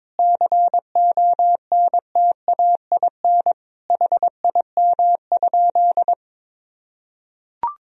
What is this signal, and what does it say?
Morse code, heard as a keyed tone, about 22 wpm, 700 Hz. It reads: CONTAIN HIM?
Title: Sentences from Top 1000 Words - Review Part 2 - 22wpm